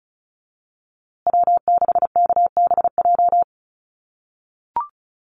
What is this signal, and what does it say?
Morse code, heard as a keyed tone, about 34 wpm, 700 Hz. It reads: W6KBJ
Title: USA Call Signs Repeated 3x 35wpm